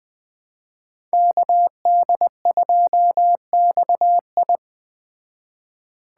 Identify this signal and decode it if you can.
Morse code, heard as a keyed tone, about 20 wpm, 700 Hz. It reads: KD2XI